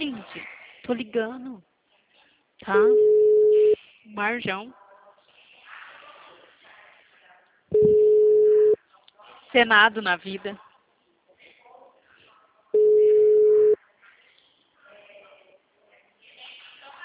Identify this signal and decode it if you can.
Ringback tone (Europe, 425 Hz), with speech.